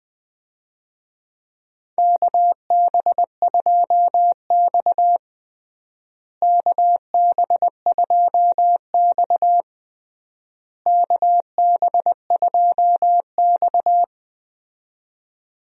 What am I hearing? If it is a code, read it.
Morse code, heard as a keyed tone, about 21 wpm, 700 Hz. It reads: KB2X KB2X KB2X
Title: USA Call Signs - 4 Characters - Repeated 3x - 20wpm